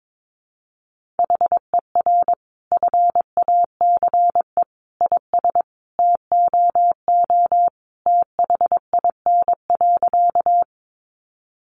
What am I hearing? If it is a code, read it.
Morse code, heard as a keyed tone, about 22 wpm, 700 Hz. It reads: HER FACE IS TOO THIN.
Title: Sentences from Top 1200 Words 22wpm